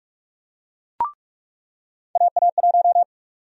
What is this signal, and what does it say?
Morse code, heard as a keyed tone, about 46 wpm, 700 Hz. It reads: AA1
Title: USA Call Sign Prefixes - 45wpm